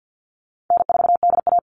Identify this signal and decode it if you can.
Morse code, heard as a keyed tone, about 48 wpm, 700 Hz. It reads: N4DA
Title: NA SOTA Chaser Callsigns 50wpm